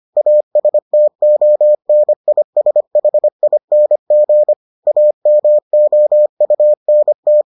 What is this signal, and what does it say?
Morse code, heard as a keyed tone, about 25 wpm, 600 Hz. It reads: ASTONISHING AMOUNT